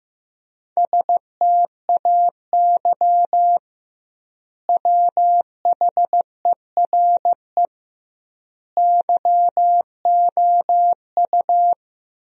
Morse code, heard as a keyed tone, about 16 wpm, 700 Hz. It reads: STAY WHERE YOU
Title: 2x Word-Spacing - Sentences from Top 500 Words - 15wpm